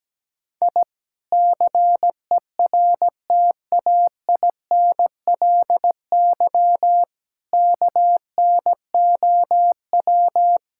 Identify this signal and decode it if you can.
Morse code, heard as a keyed tone, about 17 wpm, 700 Hz. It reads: I CERTAINLY KNOW
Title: Sentences from Top 1000 Words - Review Part 6 - 17wpm